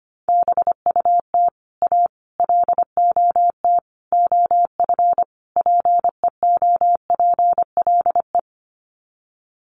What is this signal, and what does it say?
Morse code, heard as a keyed tone, about 25 wpm, 700 Hz. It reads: BUT A LOT OF PEOPLE
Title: Sets of 5 Words Encore2 25wpm